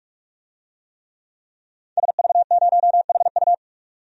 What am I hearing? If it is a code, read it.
Morse code, heard as a keyed tone, about 46 wpm, 700 Hz. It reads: SV0HU
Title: DXCC-100 Call Signs - 5 Characters - 45wpm